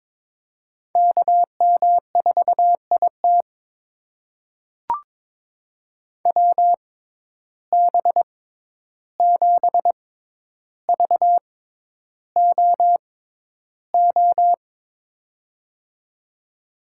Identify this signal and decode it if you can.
Morse code, heard as a keyed tone, about 21 wpm, spaced as for 11 wpm, 700 Hz. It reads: KM4IT WB7VOO
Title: USA Call Signs - 6x Character Spacing - 22wpm